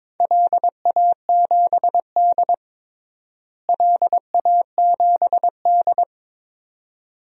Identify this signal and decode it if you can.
Morse code, heard as a keyed tone, about 22 wpm, 700 Hz. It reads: LA7D LA7D
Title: DXCC-100 Call Signs - 4 Characters - Repeated 2x - 22wpm